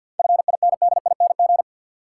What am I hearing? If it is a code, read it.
Morse code, heard as a keyed tone, about 53 wpm, 700 Hz. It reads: FINDING
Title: Sentences from Top 1100 Words - New Word Drill Alt - 50wpm